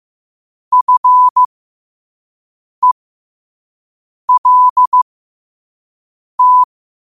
Morse code, heard as a keyed tone, about 15 wpm, spaced as for 5 wpm, 1000 Hz. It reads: FELT